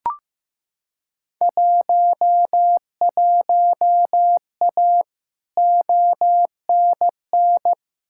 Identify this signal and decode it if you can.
Morse code, heard as a keyed tone, about 15 wpm, 700 Hz. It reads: 11A ONN
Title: Field Day Exchanges Encore - 15wpm